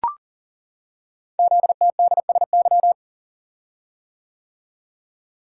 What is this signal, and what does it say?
Morse code, heard as a keyed tone, about 38 wpm, 700 Hz. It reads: ZTDSY
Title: Code Groups - 5 Letters - 40wpm